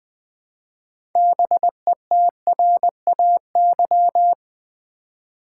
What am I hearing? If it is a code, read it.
Morse code, heard as a keyed tone, about 20 wpm, 700 Hz. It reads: BETRAY
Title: Build A Word - 3 Letters At A Time - 6-Letter Words - 20wpm